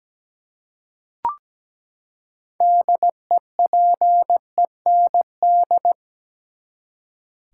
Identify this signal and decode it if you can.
Morse code, heard as a keyed tone, about 17 wpm, 700 Hz. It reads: DEPEND
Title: Six Letter Words - 17wpm